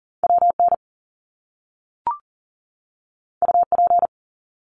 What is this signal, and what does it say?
Morse code, heard as a keyed tone, about 40 wpm, 700 Hz. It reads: WN UP
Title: Top Two Letter Combinations in Words 40wpm